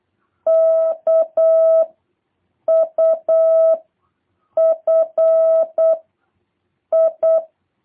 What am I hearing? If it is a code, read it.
Morse code, heard as a keyed tone, about 8 wpm, 650 Hz. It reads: K U F I